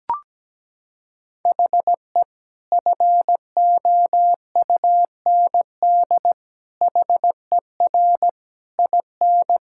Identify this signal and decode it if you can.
Morse code, heard as a keyed tone, about 17 wpm, 700 Hz. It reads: HE FOUND HER IN
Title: Sentences from Top 1600 Words 17wpm